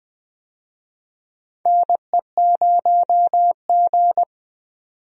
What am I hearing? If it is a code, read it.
Morse code, heard as a keyed tone, about 20 wpm, 700 Hz. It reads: NE0G
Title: USA Call Signs - 4 Characters - 20wpm